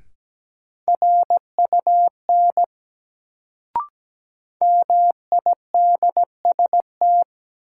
Morse code, heard as a keyed tone, about 17 wpm, 700 Hz. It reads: RUN MIDST
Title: Sets of 4 Words - New Word Drill - 17wpm